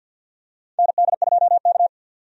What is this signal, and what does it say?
Morse code, heard as a keyed tone, about 50 wpm, 700 Hz. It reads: ND2K
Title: USA Call Signs - 4 Characters - 50wpm